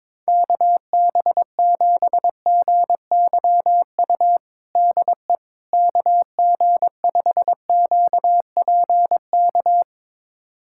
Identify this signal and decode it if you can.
Morse code, heard as a keyed tone, about 22 wpm, 700 Hz. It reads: KB7GYU DE KG5QPK
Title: USA Callsign DE Callsign - 6 Characters - 22wpm